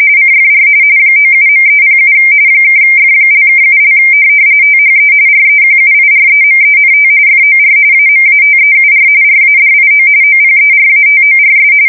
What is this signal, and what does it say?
RTTY, about 45.45 baud. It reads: CFR 97.307 (F)(7) THE REQUIREMENT FOR AMATEUR RADIO OPERATORS
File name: fsk170 / teletype-RTTY